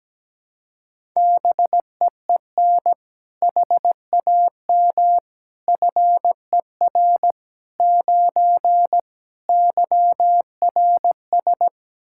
Morse code, heard as a keyed tone, about 17 wpm, 700 Hz. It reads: BEEN HAM FER 9 YRS
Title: Practice QSO - Exchange 3 - More Info - 17wpm